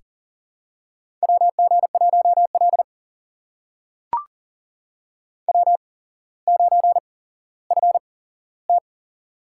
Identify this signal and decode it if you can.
Morse code, heard as a keyed tone, about 39 wpm, spaced as for 17 wpm, 700 Hz. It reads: WG1L W9FT
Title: USA Call Signs - 4 Characters - 8x Character Spacing - 40wpm